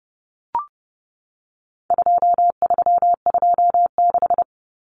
Morse code, heard as a keyed tone, about 30 wpm, 700 Hz. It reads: 2326
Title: Common Cut Numbers - 4-Digit 30wpm